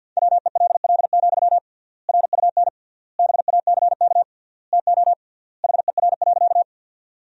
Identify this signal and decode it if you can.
Morse code, heard as a keyed tone, about 49 wpm, 700 Hz. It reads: WELL, RUN BACK TO HER.
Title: 3x Word-Spacing - Sentences from Top 300 Words - 50wpm